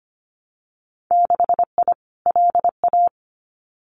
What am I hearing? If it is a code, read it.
Morse code, heard as a keyed tone, about 25 wpm, 700 Hz. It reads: 6I LA
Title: Winter Field Day Exchanges Encore 25wpm